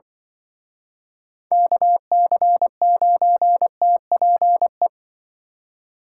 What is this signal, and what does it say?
Morse code, heard as a keyed tone, about 24 wpm, 700 Hz. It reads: KC9TPE
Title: Speed Racing - USA Call Signs 20wpm